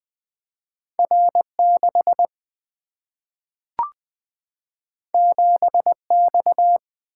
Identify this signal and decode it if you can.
Morse code, heard as a keyed tone, about 20 wpm, 700 Hz. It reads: R6 7X
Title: Single-Letter Single-Number 20wpm